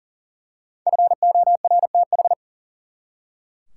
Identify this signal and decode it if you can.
Morse code, heard as a keyed tone, about 37 wpm, 700 Hz. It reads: FORTH
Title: Five Letter Words - 40wpm